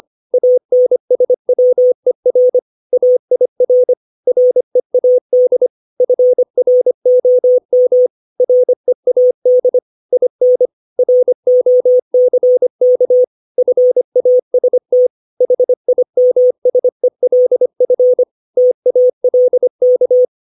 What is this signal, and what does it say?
Morse code, heard as a keyed tone, about 25 wpm, 500 Hz. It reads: ANSWER AIR READ FROM READ IN ROCK FAST HIMSELF TALK